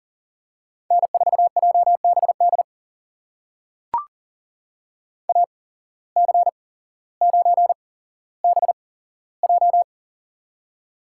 Morse code, heard as a keyed tone, about 40 wpm, spaced as for 17 wpm, 700 Hz. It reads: N4JBD AC9BJ
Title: USA Call Signs - 5 Characters - 8x Character Spacing - 40wpm